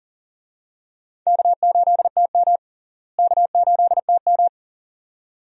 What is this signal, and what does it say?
Morse code, heard as a keyed tone, about 40 wpm, 700 Hz. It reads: K8TM K8TM
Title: USA Call Signs - 4 Characters - Repeated 2x - 40wpm